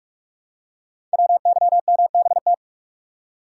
Morse code, heard as a keyed tone, about 44 wpm, 700 Hz. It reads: WYMBT
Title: Code Groups - 5 Letters - 45wpm